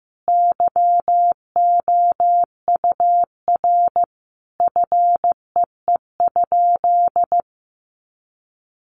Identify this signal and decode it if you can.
Morse code, heard as a keyed tone, about 15 wpm, 700 Hz. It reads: YOUR FEE?
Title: Sentences from Top 1500 Words 15wpm